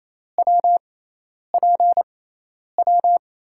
Morse code, heard as a keyed tone, about 28 wpm, spaced as for 10 wpm, 700 Hz. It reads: WPW
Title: USA Call Signs - 6 Characters - 6x Character Spacing - 28wpm